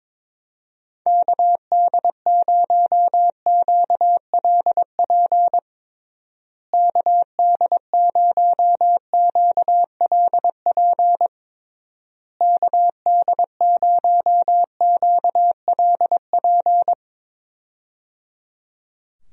Morse code, heard as a keyed tone, about 22 wpm, 700 Hz. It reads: KD0QLP KD0QLP KD0QLP
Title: USA Call Signs - 6 Characters - Repeated 3x - 22wpm